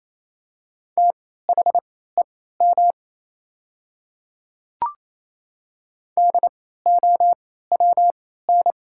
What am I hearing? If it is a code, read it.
Morse code, heard as a keyed tone, about 28 wpm, spaced as for 16 wpm, 700 Hz. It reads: THEM DOWN
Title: 3x Character Spacing - Top 100 Words - 28wpm